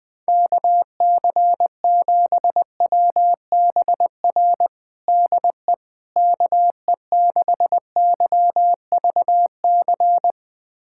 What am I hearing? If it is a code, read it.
Morse code, heard as a keyed tone, about 20 wpm, 700 Hz. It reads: KC7WBR DE KE6YVC
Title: USA Callsign DE Callsign - 6 Characters - 20wpm